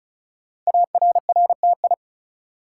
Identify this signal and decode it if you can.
Morse code, heard as a keyed tone, about 33 wpm, 690 Hz. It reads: ARRTI
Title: Code Groups - 5 Letters - 35wpm